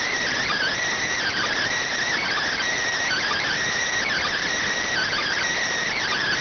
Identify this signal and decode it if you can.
Emergency Alert System (SAME) tones (header bursts), heard as harsh buzzing bursts. It is heard from 0.9 to 5.5 s.